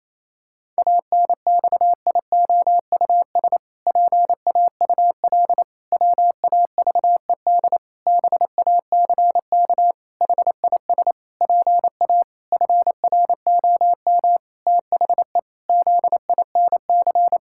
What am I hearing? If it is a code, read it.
Morse code, heard as a keyed tone, about 28 wpm, 700 Hz. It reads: ANXIOUS PAUL WAVED BACK HIS PA FROM THE ZINC
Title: Pangram Challenge v2 - 28wpm